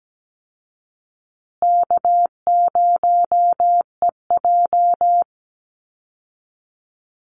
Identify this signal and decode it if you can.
Morse code, heard as a keyed tone, about 17 wpm, 700 Hz. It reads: K0EJ
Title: Call Signs Repeated 2x - 17wpm